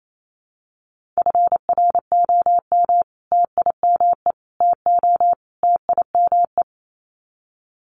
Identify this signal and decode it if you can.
Morse code, heard as a keyed tone, about 28 wpm, 700 Hz. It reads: FROM TIME TO TIME
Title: Sets of 4 Words 28wpm